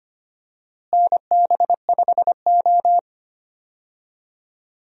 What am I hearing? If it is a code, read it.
Morse code, heard as a keyed tone, about 25 wpm, 700 Hz. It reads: NB5O
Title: USA Call Signs - 4 Characters - Repeated 2x - 25wpm